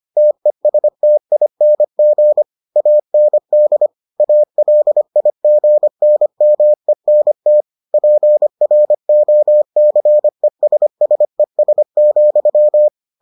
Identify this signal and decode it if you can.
Morse code, heard as a keyed tone, about 25 wpm, 600 Hz. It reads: TESTING AND ALIGNMENT PROCESSES,